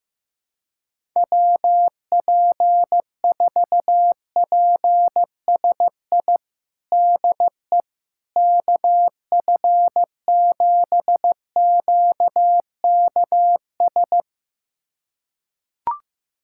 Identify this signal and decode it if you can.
Morse code, heard as a keyed tone, about 15 wpm, 700 Hz. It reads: WP4PSI DE KF7QKS
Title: USA Callsign DE Callsign - 6 Characters - 15wpm